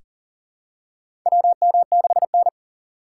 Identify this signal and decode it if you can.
Morse code, heard as a keyed tone, about 37 wpm, 700 Hz. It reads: WM6N
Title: USA Call Signs - 4 Characters - 3x Character Spacing - 40wpm